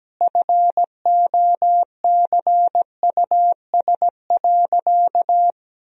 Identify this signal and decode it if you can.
Morse code, heard as a keyed tone, about 17 wpm, 700 Hz. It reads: FOCUS.
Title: Sentences from Top 1000 Words - Review Part 4 - 17wpm